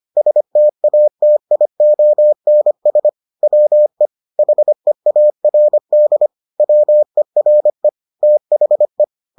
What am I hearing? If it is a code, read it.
Morse code, heard as a keyed tone, about 25 wpm, 600 Hz. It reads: STATIONS WE HEARD WERE THE